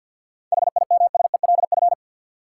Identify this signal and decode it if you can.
Morse code, heard as a keyed tone, about 53 wpm, 700 Hz. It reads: HIMSELF
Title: Sentences from Top 1500 Words - New Word Drill - 50wpm